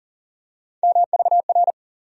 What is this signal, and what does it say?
Morse code, heard as a keyed tone, about 40 wpm, 700 Hz. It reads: MVR
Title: USA Call Sign Postfixes - 3 Letters 40wpm